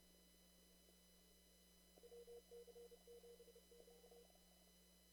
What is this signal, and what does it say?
Morse code, heard as a keyed tone, about 29 wpm, 440 Hz, with noise. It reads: WC7Q
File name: CW / kt5j wpx cw-op-k5pi